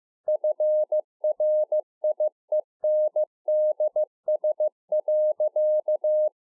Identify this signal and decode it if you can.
Morse code, heard as a keyed tone, about 15 wpm, 610 Hz. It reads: FRIENDS.